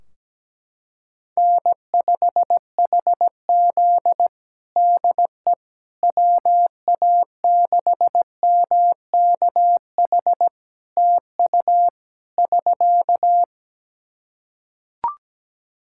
Morse code, heard as a keyed tone, about 17 wpm, 700 Hz. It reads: N5HZ DE WA6MKH TU <SK>
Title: Practice QSO - Exchange 4 - Ending - 17wpm